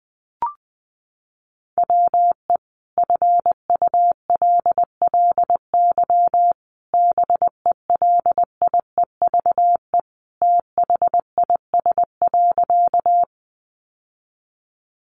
Morse code, heard as a keyed tone, about 20 wpm, 700 Hz. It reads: WE FULLY BELIEVE THIS.
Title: Sentences from Top 1300 Words 20wpm